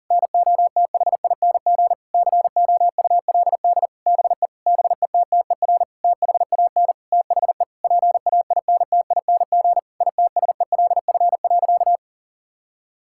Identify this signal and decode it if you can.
Morse code, heard as a keyed tone, about 40 wpm, 700 Hz. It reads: NOTHING COULD BE BETTER THAN THE PAINTING ITSELF.
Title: Sentences from Top 1000 Words - Review Part 6 - 40wpm